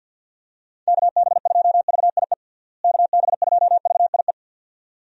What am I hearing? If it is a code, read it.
Morse code, heard as a keyed tone, about 47 wpm, 700 Hz. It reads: KB2VIE KB2VIE
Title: USA Call Signs - 6 Characters - Repeated 2x - 50wpm